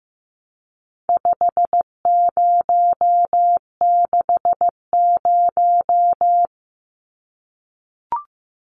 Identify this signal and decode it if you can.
Morse code, heard as a keyed tone, about 15 wpm, 700 Hz. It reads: 5060
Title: Numbers - 4-Digit 15wpm